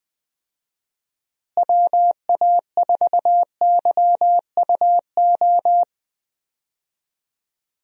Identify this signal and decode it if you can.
Morse code, heard as a keyed tone, about 20 wpm, 700 Hz. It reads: WA4YUO
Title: USA Call Signs - 6 Characters - Repeated 2x - 20wpm